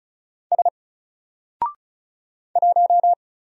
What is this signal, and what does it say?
Morse code, heard as a keyed tone, about 36 wpm, 700 Hz. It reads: S 1
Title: Single Character (Letter:Number:Punctuation:Prosign) 35wpm